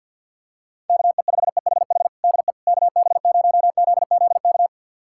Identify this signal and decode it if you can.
Morse code, heard as a keyed tone, about 52 wpm, 700 Hz. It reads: KE5EHH DE KB0ZZK
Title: USA Callsign DE Callsign - 6 Characters - 50wpm